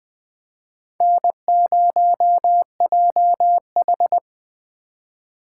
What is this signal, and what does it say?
Morse code, heard as a keyed tone, about 21 wpm, 700 Hz. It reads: N0JH
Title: Speed Racing - USA Call Signs 17wpm